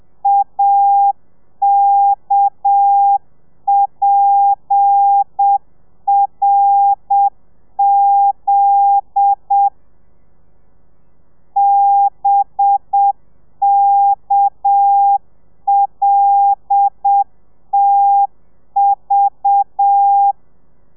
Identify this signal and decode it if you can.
Morse code, heard as a keyed tone, about 7 wpm, 800 Hz. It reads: AKPRZ BKLTV
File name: telegrafia / i2tfj / Lez10 30